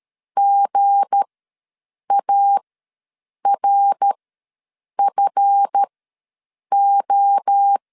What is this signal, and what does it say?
Morse code, heard as a keyed tone, about 13 wpm, spaced as for 7 wpm, 800 Hz. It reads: GARFO
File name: LEZIONI / 9 28 ws 12wpm 3354